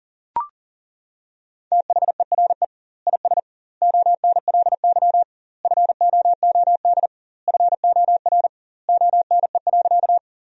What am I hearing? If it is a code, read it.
Morse code, heard as a keyed tone, about 40 wpm, 700 Hz. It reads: THERE IS ONLY FOOD FOR ONE.
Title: 2x Word-Spacing - Sentences from Top 300 Words - 40wpm